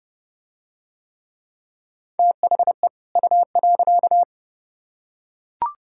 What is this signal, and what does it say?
Morse code, heard as a keyed tone, about 28 wpm, 700 Hz. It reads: THE U.